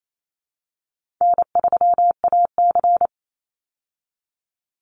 Morse code, heard as a keyed tone, about 28 wpm, 700 Hz. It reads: N3AC
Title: Easy Call Signs Repeated 2x - 28wpm